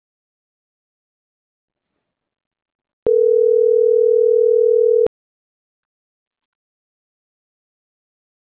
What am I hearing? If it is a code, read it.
Ringback tone (North America), 440 + 480 Hz.